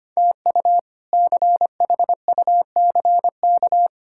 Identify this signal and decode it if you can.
Morse code, heard as a keyed tone, about 25 wpm, 700 Hz. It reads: TU CHUCK
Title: K1USN SST QSO Practice - 25wpm